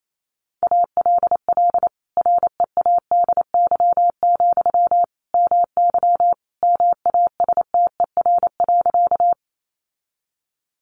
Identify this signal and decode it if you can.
Morse code, heard as a keyed tone, about 28 wpm, 700 Hz. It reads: ALL READY, MY MASTER.
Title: Sentences from Top 1300 Words 28wpm